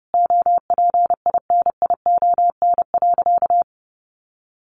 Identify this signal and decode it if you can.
Morse code, heard as a keyed tone, about 30 wpm, 700 Hz. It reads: OPINION.
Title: Sentences from Top 1200 Words 30wpm